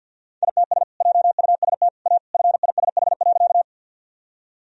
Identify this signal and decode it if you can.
Morse code, heard as a keyed tone, about 52 wpm, 700 Hz. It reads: ITS JUST A FISH.
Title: Sentences from Top 1000 Words - Review Part 1 - 50wpm